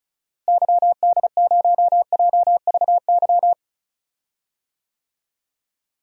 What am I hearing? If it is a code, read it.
Morse code, heard as a keyed tone, about 36 wpm, 700 Hz. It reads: YD0JVY
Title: DXCC-100 Call Signs - 6 Characters - Repeated 2x - 35wpm